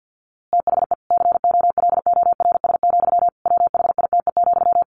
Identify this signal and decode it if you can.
Morse code, heard as a keyed tone, about 50 wpm, 700 Hz. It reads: THE COLORS, WHITE,
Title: A Sound Of Thunder 50wpm